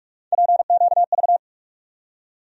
Morse code, heard as a keyed tone, about 45 wpm, 700 Hz. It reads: PQV
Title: USA Call Sign Postfixes - 3 Letters 45wpm